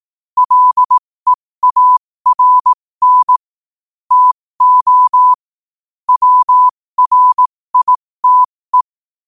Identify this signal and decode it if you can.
Morse code, heard as a keyed tone, about 18 wpm, spaced as for 15 wpm, 1000 Hz. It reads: LEARN TO WRITE
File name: qotdfiles / 15wpm